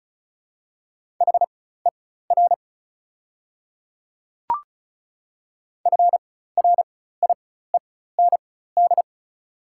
Morse code, heard as a keyed tone, about 36 wpm, spaced as for 16 wpm, 700 Hz. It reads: HER FRIEND
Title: 4x Character Spacing - Top 300 Words - 35wpm